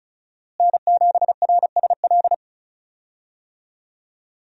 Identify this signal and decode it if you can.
Morse code, heard as a keyed tone, about 34 wpm, 700 Hz. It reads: N7RSL
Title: USA Call Signs - 5 Characters - Repeated 3x - 35wpm